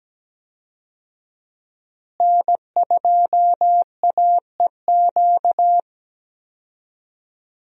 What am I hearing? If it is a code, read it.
Morse code, heard as a keyed tone, about 17 wpm, 700 Hz. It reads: N2AEQ
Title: USA Call Signs - 5 Characters - Repeated 3x - 17wpm